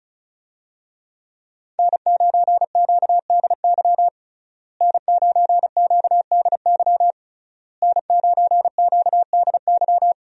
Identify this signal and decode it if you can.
Morse code, heard as a keyed tone, about 36 wpm, 700 Hz. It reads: N9QDY N9QDY N9QDY
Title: USA Call Signs - 5 Characters - Repeated 3x - 35wpm